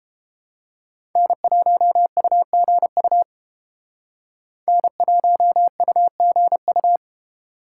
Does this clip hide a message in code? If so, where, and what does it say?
Yes: Morse code from 1.2 to 7.0 s, heard as a keyed tone, about 31 wpm, 700 Hz. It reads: N1UGU N1UGU